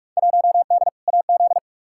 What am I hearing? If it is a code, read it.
Morse code, heard as a keyed tone, about 46 wpm, 700 Hz. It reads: 1D AZ
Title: Field Day Exchanges - 45wpm